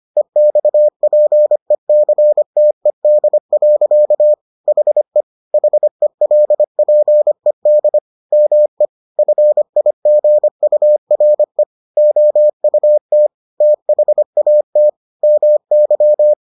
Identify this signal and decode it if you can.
Morse code, heard as a keyed tone, about 25 wpm, 600 Hz. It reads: EXPECTED. HE HELPED ME FIGURE OUT THAT MY